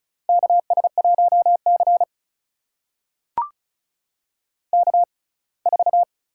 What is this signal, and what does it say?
Morse code, heard as a keyed tone, about 36 wpm, spaced as for 24 wpm, 700 Hz. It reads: KS1C K4
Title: USA Call Signs - 6x Character Spacing - 35wpm